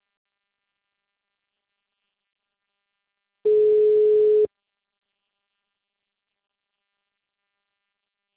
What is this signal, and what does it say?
Ringback tone (Europe, 425 Hz).